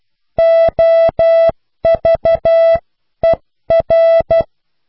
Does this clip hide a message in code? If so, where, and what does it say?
Yes: Morse code from 0.4 to 4.4 s, heard as a keyed tone, about 12 wpm, 660 Hz. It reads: OVER